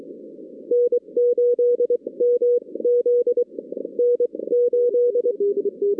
Morse code, heard as a keyed tone, about 23 wpm, spaced as for 19 wpm, 480 Hz. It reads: N8MZ N8
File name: 2016 arrl dx cw / 2016-ARRL-DX-CW-K5TR- - 0-46